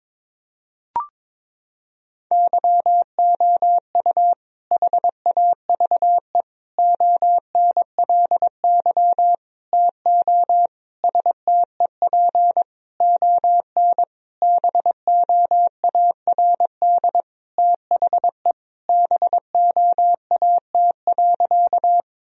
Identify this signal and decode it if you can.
Morse code, heard as a keyed tone, about 22 wpm, 700 Hz. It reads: YOU HAVE ONLY TO STEP ON BOARD THE BOAT.
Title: Sentences from Top 1000 Words - Review Part 4 - 22wpm